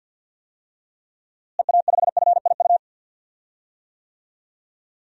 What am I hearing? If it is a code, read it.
Morse code, heard as a keyed tone, about 52 wpm, 700 Hz. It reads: EA5FIU
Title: DXCC-100 Call Signs - 6 Characters - Repeated 2x - 50wpm